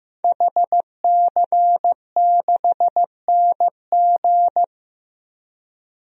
Morse code, heard as a keyed tone, about 15 wpm, 700 Hz. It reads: HC6NG
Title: DXCC-100 Call Signs - 5 Characters - 15wpm